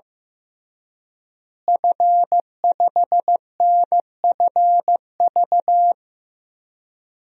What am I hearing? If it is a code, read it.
Morse code, heard as a keyed tone, about 15 wpm, 700 Hz. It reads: F5NFV
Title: DXCC-100 Call Signs - 5 Characters - Repeated 3x - 15wpm